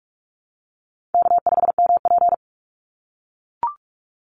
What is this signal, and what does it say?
Morse code, heard as a keyed tone, about 44 wpm, 700 Hz. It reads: K5MP
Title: Easy Call Signs - 45wpm